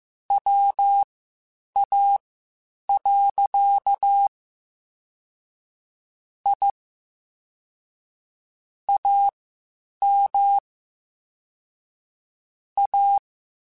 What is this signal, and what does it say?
Morse code, heard as a keyed tone, about 15 wpm, spaced as for 8 wpm, 800 Hz. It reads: WA. I AM A
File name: QSOs Spaced-mp3s/kc7haa @13 sp